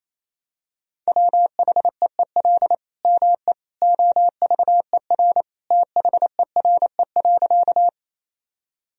Morse code, heard as a keyed tone, about 28 wpm, 700 Hz. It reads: WHEEL ME OVER THERE.
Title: Sentences from Top 1000 Words - Review Part 3 - 28wpm